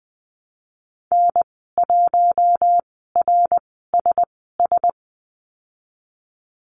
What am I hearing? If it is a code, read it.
Morse code, heard as a keyed tone, about 20 wpm, spaced as for 15 wpm, 700 Hz. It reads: N1RSS
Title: 076 Lesson 13g - Letter L - Call Signs 20wpm